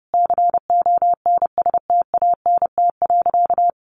Morse code, heard as a keyed tone, about 30 wpm, 700 Hz. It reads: CONSTANT.
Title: Sentences from Top 1500 Words 30wpm